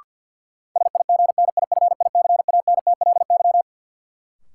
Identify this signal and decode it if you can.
Morse code, heard as a keyed tone, about 52 wpm, 700 Hz. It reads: SIGNIFICANTLY
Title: Sentences from Top 1500 Words - New Word Drill Alt - 50wpm